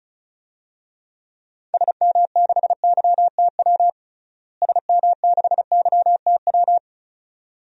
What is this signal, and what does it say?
Morse code, heard as a keyed tone, about 34 wpm, 700 Hz. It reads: SM6YTW SM6YTW
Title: DXCC-100 Call Signs - 6 Characters - Repeated 2x - 35wpm